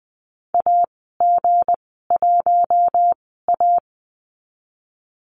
Morse code, heard as a keyed tone, about 20 wpm, spaced as for 14 wpm, 700 Hz. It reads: AG1A
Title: 186 Lesson 30g - Letter b - Call Signs 20wpm